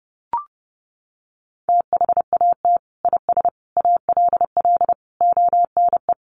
Morse code, heard as a keyed tone, about 30 wpm, 700 Hz. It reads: THAT IS ALL ONE
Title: Sentences from Top 200 Words 30wpm